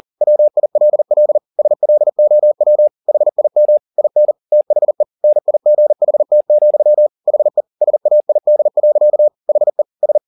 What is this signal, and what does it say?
Morse code, heard as a keyed tone, about 40 wpm, 600 Hz. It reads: WILL SLOW HIM IN THE NIGHT, HE SAID. HE S
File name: the old man and the sea 2 40wpm